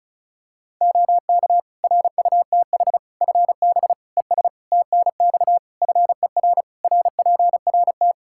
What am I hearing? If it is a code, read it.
Morse code, heard as a keyed tone, about 36 wpm, 700 Hz. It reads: OK RUTH FB ES TNX FER RPRT
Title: Practice QSO - Exchange 2 - Info - 35wpm